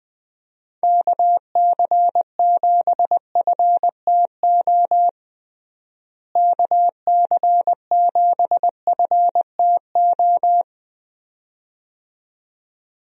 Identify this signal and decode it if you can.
Morse code, heard as a keyed tone, about 20 wpm, 700 Hz. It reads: KC7FTO KC7FTO
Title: USA Call Signs - 6 Characters - Repeated 2x - 20wpm